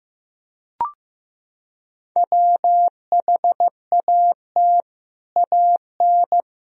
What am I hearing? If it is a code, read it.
Morse code, heard as a keyed tone, about 15 wpm, 700 Hz. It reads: WHAT AN
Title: Sentences from Top 1000 Words - Review Part 3 - 15wpm